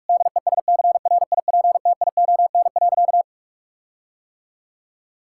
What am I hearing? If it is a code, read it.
Morse code, heard as a keyed tone, about 41 wpm, 700 Hz. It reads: DESCRIPTION.
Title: Sentences from Top 1600 Words 45wpm